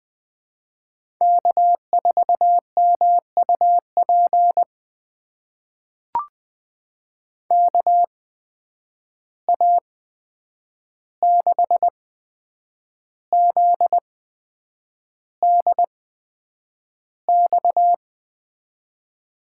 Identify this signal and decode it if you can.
Morse code, heard as a keyed tone, about 20 wpm, spaced as for 8 wpm, 700 Hz. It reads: K4MUP KA6ZDX
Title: USA Call Signs - 8x Character Spacing - 20wpm